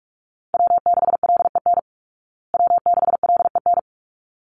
Morse code, heard as a keyed tone, about 44 wpm, 700 Hz. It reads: W6LEN W6LEN
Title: Call Signs Repeated 3x - 45wpm